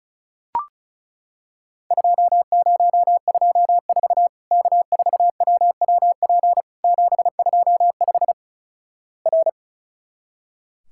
Morse code, heard as a keyed tone, about 36 wpm, 700 Hz. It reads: 2024 K4WWP 725 R
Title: Worked All Europe DX Contest - QTC Exchange - 35wpm